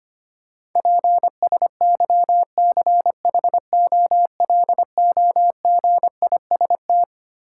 Morse code, heard as a keyed tone, about 25 wpm, 700 Hz. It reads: PSYCHOLOGIST
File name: Twelve Letter Words - 25wpm